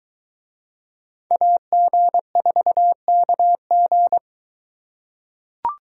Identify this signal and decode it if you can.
Morse code, heard as a keyed tone, about 22 wpm, 700 Hz. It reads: AG4KG